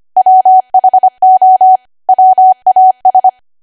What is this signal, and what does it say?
Morse code, heard as a keyed tone, about 25 wpm, 750 Hz. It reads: WHO WAS
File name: cwa / W1AW-Level3 / 25WPM